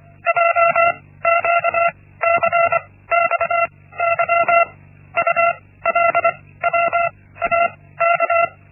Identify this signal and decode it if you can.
Morse code, heard as a keyed tone, about 24 wpm, spaced as for 16 wpm, 680 Hz. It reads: JQCXY ULWAK
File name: Linksmakalnis-1